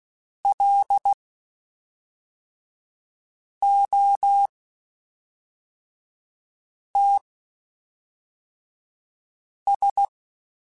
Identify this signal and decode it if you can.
Morse code, heard as a keyed tone, about 16 wpm, spaced as for 3 wpm, 800 Hz. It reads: LOTS
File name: Spaced Characters / Spaced Characters-mp3s / Weather5 12 sp @5